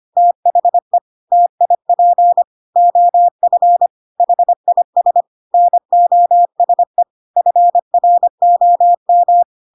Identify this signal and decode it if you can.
Morse code, heard as a keyed tone, about 25 wpm, 700 Hz. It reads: THE TIP OF HIS NOSE FROM
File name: Pinocchio / Pinocchio ch1 25 25 ik1hge